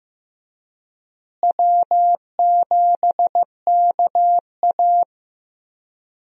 Morse code, heard as a keyed tone, about 15 wpm, 700 Hz. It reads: W7KA
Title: USA Call Signs - 4 Characters - Repeated 2x - 15wpm